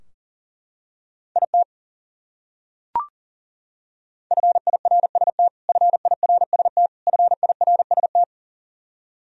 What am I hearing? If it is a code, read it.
Morse code, heard as a keyed tone, about 40 wpm, 700 Hz. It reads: IT FIRST FIRST FIRST
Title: Top 100 Words Repeated 40wpm